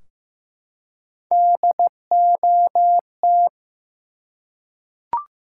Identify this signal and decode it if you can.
Morse code, heard as a keyed tone, about 15 wpm, 700 Hz. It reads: DOT
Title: Three Letter Words - 15wpm